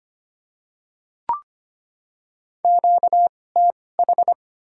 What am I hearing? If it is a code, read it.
Morse code, heard as a keyed tone, about 25 wpm, spaced as for 18 wpm, 700 Hz. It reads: QTH
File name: Top 20 QSO Elements - 2x Character Spacing - 25wpm